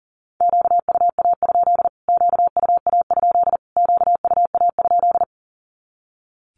Morse code, heard as a keyed tone, about 40 wpm, 700 Hz. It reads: QUA? QUA? QUA?
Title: Q-Codes Repeated 3x 40wpm